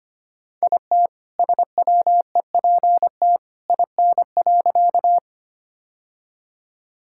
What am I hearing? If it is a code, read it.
Morse code, heard as a keyed tone, about 25 wpm, 700 Hz. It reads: IT SWEPT IN.
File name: practice / 25wpm / All-Age Appropriate Jokes - 25wpm